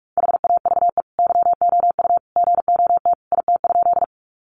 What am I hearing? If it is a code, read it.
Morse code, heard as a keyed tone, about 44 wpm, 700 Hz. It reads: HAVE YOU GOT IT?
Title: Sentences from Top 300 Words 45wpm